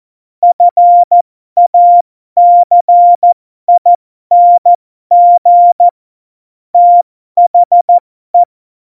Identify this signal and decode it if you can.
Morse code, heard as a keyed tone, about 14 wpm, spaced as for 12 wpm, 700 Hz. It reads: FACING THE
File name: Morsekurs / Witze